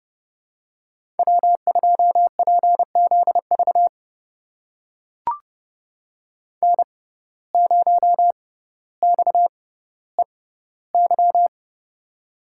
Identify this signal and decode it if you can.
Morse code, heard as a keyed tone, about 32 wpm, spaced as for 16 wpm, 700 Hz. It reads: W2PZV N0XEY
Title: USA Call Signs - 5 Characters - 6x Character Spacing - 30wpm